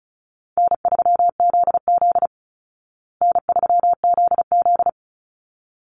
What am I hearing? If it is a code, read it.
Morse code, heard as a keyed tone, about 34 wpm, 700 Hz. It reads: N3ZZ N3ZZ
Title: Easy Call Signs Repeated 2x - 35wpm